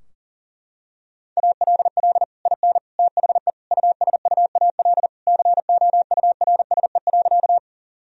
Morse code, heard as a keyed tone, about 40 wpm, 700 Hz. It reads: ALL IN THE USUAL COURSE.
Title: Sentences from Top 1000 Words - Review Part 2 - 40wpm